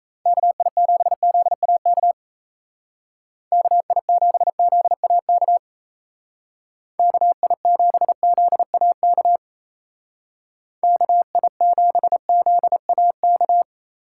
Morse code, heard as a keyed tone, about 36 wpm, 700 Hz. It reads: KI7ZAK KI7ZAK KI7ZAK KI7ZAK